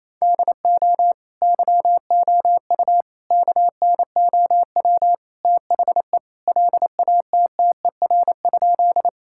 Morse code, heard as a keyed tone, about 27 wpm, 700 Hz. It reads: DO YOU KNOW THE LATTER?
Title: Sentences from Top 1600 Words 28wpm